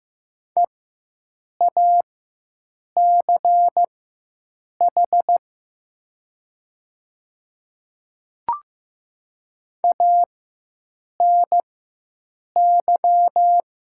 Morse code, heard as a keyed tone, about 15 wpm, spaced as for 7 wpm, 700 Hz. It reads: EACH ANY
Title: 4x Character Spacing - Top 200 Words - 15wpm